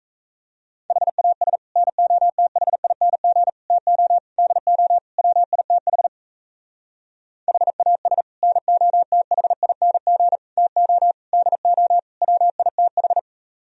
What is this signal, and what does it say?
Morse code, heard as a keyed tone, about 36 wpm, 700 Hz. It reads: HAS NOTHING TO DO WITH HAS NOTHING TO DO WITH